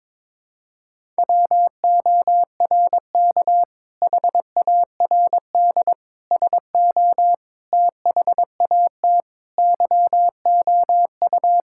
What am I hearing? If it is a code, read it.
Morse code, heard as a keyed tone, about 22 wpm, 700 Hz. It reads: WORK HARD SO THAT YOU
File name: practice / 22wpm / Sentences with 2-Word Phrases - 22wpm